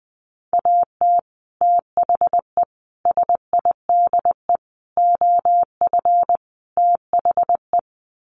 Morse code, heard as a keyed tone, about 20 wpm, 700 Hz. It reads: AT THE SIDE OF THE
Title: Sets of 5 Words Encore2 20wpm